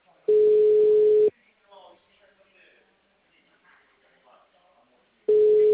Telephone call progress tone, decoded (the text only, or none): ringback tone